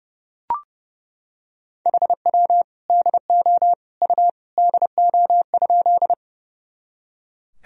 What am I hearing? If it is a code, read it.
Morse code, heard as a keyed tone, about 32 wpm, 700 Hz. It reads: HW DO U DO?
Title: Sentences from Top 100 Words and Abbreviations 30wpm